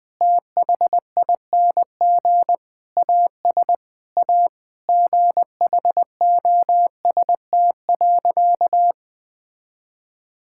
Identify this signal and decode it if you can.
Morse code, heard as a keyed tone, about 20 wpm, 700 Hz. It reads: THING AS A GHOST.
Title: Sentences with 8-Word Phrases - Spoken Beginning - 20wpm